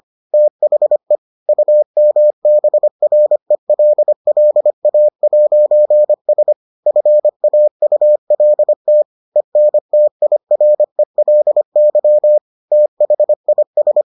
Morse code, heard as a keyed tone, about 25 wpm, 600 Hz. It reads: THE UMBRELLA'S FAULT ENTIRELY THIS